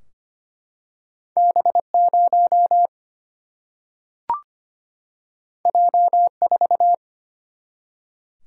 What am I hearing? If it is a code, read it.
Morse code, heard as a keyed tone, about 25 wpm, 700 Hz. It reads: B 0 J 4
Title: Single-Letter Single-Number 25wpm